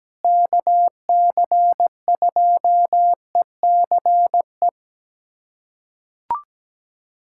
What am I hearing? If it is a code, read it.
Morse code, heard as a keyed tone, about 17 wpm, 700 Hz. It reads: KC2ECE
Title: USA Callsign DE Callsign 17wpm